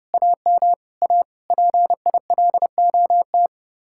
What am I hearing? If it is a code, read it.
Morse code, heard as a keyed tone, about 36 wpm, 700 Hz. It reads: AM A PILOT
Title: Speed-Racing Common QSO Phrases - 30wpm